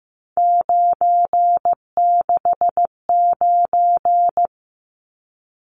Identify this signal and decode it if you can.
Morse code, heard as a keyed tone, about 15 wpm, 700 Hz. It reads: 969
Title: Numbers - 3-Digit 15wpm